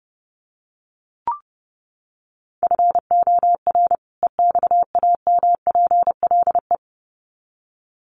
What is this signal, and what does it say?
Morse code, heard as a keyed tone, about 30 wpm, 700 Hz. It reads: FOR EXAMPLE
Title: Sets of 2 Words 30wpm